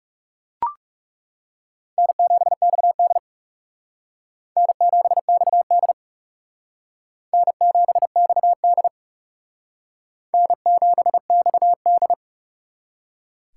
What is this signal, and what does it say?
Morse code, heard as a keyed tone, about 37 wpm, 700 Hz. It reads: N7XD N7XD N7XD N7XD
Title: Speed Racing - USA 1x2 Call Signs 30wpm